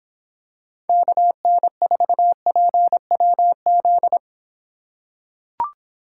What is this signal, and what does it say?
Morse code, heard as a keyed tone, about 26 wpm, 700 Hz. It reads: KN4PWZ